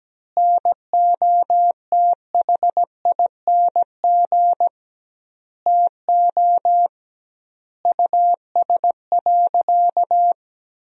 Morse code, heard as a keyed tone, about 17 wpm, 700 Hz. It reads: NOTHING TO US.
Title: 2x Word-Spacing - Sentences from Top 500 Words - 17wpm